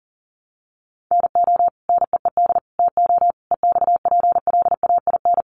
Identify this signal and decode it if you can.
Morse code, heard as a keyed tone, about 40 wpm, 700 Hz. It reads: NO NEED TO EXPLAIN
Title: Sentences from Top 1100 Words 40wpm